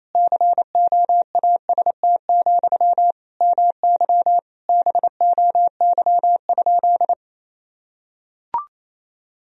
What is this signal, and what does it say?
Morse code, heard as a keyed tone, about 28 wpm, 700 Hz. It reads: COAST, MY BOY?
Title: Sentences from Top 1000 Words - Review Part 4 - 28wpm